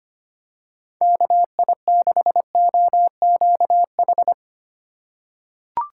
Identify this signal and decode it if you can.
Morse code, heard as a keyed tone, about 25 wpm, 700 Hz. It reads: KI6OQH
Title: USA Call Signs - 6 Characters - 4x Character Spacing - 25wpm